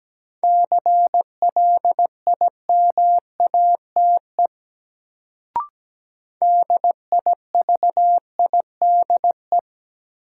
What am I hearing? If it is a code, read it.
Morse code, heard as a keyed tone, about 17 wpm, 700 Hz. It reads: CLIMATE DIVIDE
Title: Sentences from Top 1600 Words - New Word Drill Alt - 17wpm